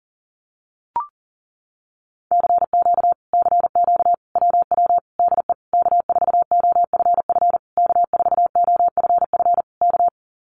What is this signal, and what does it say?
Morse code, heard as a keyed tone, about 40 wpm, 700 Hz. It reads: CQ CQ WW DE K4OFF K4OFF K
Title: Common QSO Phrases Encore 40wpm